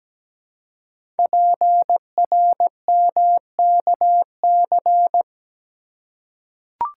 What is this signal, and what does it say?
Morse code, heard as a keyed tone, about 17 wpm, 700 Hz. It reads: PRMKC
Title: Code Groups - 5 Letters - 17wpm